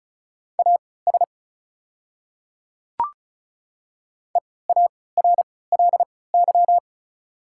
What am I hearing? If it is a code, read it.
Morse code, heard as a keyed tone, about 36 wpm, spaced as for 20 wpm, 690 Hz. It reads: AS EARLY
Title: 3x Character Spacing - Top 500 Words - 35wpm